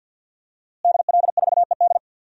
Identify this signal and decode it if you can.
Morse code, heard as a keyed tone, about 47 wpm, 700 Hz. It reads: DL4ED
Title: DXCC-100 Call Signs - 5 Characters - Repeated 2x - 50wpm